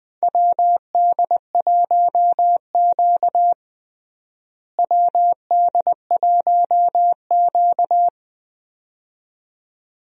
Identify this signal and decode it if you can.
Morse code, heard as a keyed tone, about 20 wpm, 700 Hz. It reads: WD1Q WD1Q
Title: USA Call Signs - 4 Characters - Repeated 2x - 20wpm